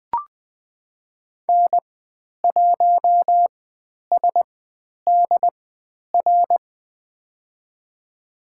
Morse code, heard as a keyed tone, about 20 wpm, spaced as for 10 wpm, 700 Hz. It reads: N1SDR
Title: 109 Lesson 18g - Letter m - Call Signs 20wpm